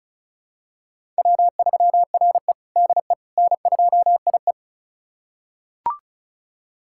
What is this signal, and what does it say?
Morse code, heard as a keyed tone, about 36 wpm, 700 Hz. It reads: W3RE DE N2IE
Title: USA Callsign DE Callsign - 4 Characters - 35wpm